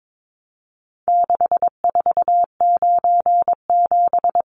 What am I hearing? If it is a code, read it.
Morse code, heard as a keyed tone, about 22 wpm, 700 Hz. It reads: 6497
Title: Numbers - 4-Digit 22wpm